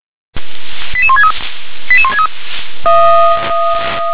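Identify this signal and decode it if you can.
Five-tone selective call (ZVEI), heard as quick runs of five tones, tones of about 70 ms. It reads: 88163 88163
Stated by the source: ZVEI five-tone selective call received on an SDR.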